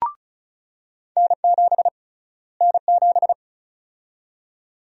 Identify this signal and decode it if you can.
Morse code, heard as a keyed tone, about 34 wpm, 700 Hz. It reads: N7 N7
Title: USA Call Sign Prefixes - Repeated 2x - 35wpm